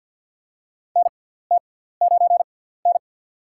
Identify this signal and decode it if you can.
Morse code, heard as a keyed tone, about 52 wpm, spaced as for 17 wpm, 700 Hz. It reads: NT9N
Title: USA Call Signs - 4 Characters - 6x Character Spacing - 50wpm